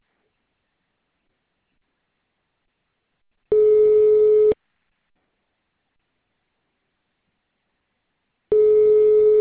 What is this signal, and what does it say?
Ringback tone (Europe, 425 Hz).